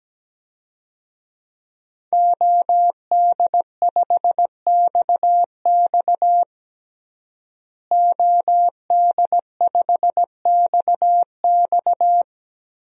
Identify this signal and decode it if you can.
Morse code, heard as a keyed tone, about 17 wpm, 700 Hz. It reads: OD5XX OD5XX
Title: DXCC-100 Call Signs - 5 Characters - Repeated 2x - 17wpm